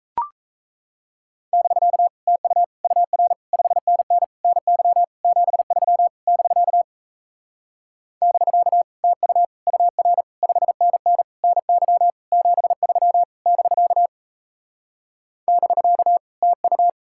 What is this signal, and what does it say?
Morse code, heard as a keyed tone, about 38 wpm, 700 Hz. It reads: <BK> TU UR 5NN NY 73 <BK> <BK> TU UR 5NN NY 73 <BK> <BK> TU